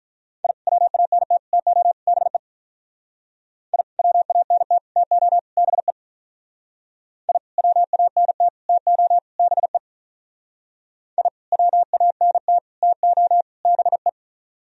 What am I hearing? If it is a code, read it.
Morse code, heard as a keyed tone, about 43 wpm, 700 Hz. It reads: I WANT TO BE I WANT TO BE I WANT TO BE I WANT TO BE
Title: Speed-Racing - Sets of 4 Words 35wpm